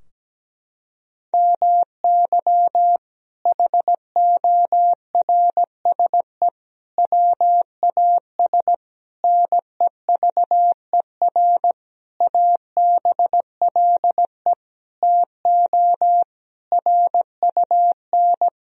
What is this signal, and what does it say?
Morse code, heard as a keyed tone, about 17 wpm, 700 Hz. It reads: MY HORSE WAS NEVER ABLE TO RUN
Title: Sentences from Top 1000 Words - Review Part 2 - 17wpm